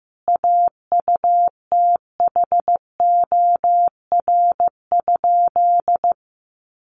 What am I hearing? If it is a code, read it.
Morse code, heard as a keyed tone, about 15 wpm, 700 Hz. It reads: AUTHOR?
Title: Sentences from Top 1200 Words 15wpm